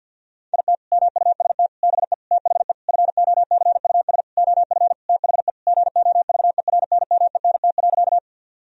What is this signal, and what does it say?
Morse code, heard as a keyed tone, about 52 wpm, 700 Hz. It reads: IT MUST BE THE FOCUS OF THE GOVERNMENT.
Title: Sentences from Top 1000 Words - Review Part 4 - 50wpm